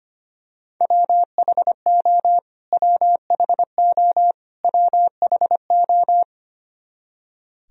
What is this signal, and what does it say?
Morse code, heard as a keyed tone, about 25 wpm, 700 Hz. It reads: WHO WHO WHO
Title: Top 100 Words Repeated 25wpm